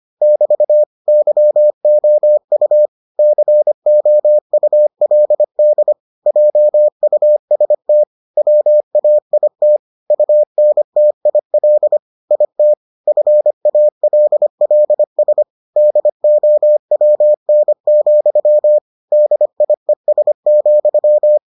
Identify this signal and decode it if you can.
Morse code, heard as a keyed tone, about 25 wpm, 600 Hz. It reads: <BT> YOU COULD JUST WAIT UNTIL IT FALLS DOWN, DIES,